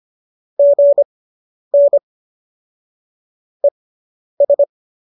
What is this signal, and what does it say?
Morse code, heard as a keyed tone, about 25 wpm, spaced as for 10 wpm, 590 Hz. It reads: GN ES